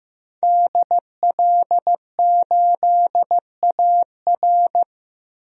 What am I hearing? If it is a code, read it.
Morse code, heard as a keyed tone, about 15 wpm, 700 Hz. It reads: DL8AR
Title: DXCC-100 Call Signs - 5 Characters - Repeated 3x - 15wpm